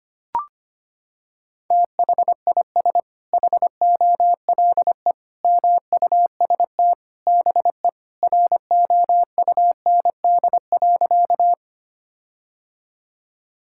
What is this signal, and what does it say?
Morse code, heard as a keyed tone, about 25 wpm, 700 Hz. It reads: THIS HOLE MUST BE ROUND.
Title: Sentences from Top 1600 Words 25wpm